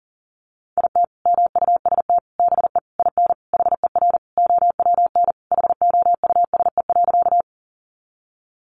Morse code, heard as a keyed tone, about 40 wpm, 700 Hz. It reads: IT MUST BE IN HER OWN HOUSE.
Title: Sentences from Top 300 Words 40wpm